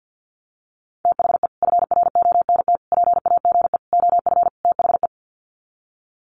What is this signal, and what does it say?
Morse code, heard as a keyed tone, about 50 wpm, 700 Hz. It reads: THE FRONT PAGE OF THE
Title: Sets of 5 Words Encore 50wpm